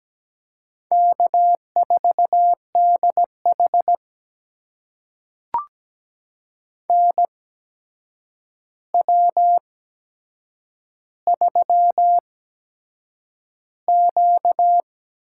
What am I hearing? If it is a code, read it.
Morse code, heard as a keyed tone, about 17 wpm, spaced as for 7 wpm, 700 Hz. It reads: K4DH NW3Q
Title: USA Call Signs - 4 Characters - 8x Character Spacing - 17wpm